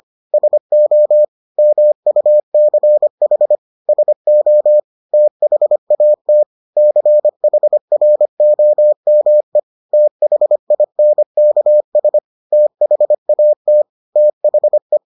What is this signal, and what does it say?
Morse code, heard as a keyed tone, about 25 wpm, 600 Hz. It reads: SO MUCH SO THAT CHROME THINKS THAT THE